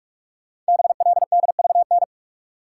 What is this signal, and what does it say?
Morse code, heard as a keyed tone, about 44 wpm, 690 Hz. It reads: BLDVN
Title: Code Groups - 5 Letter:Numbers - 45wpm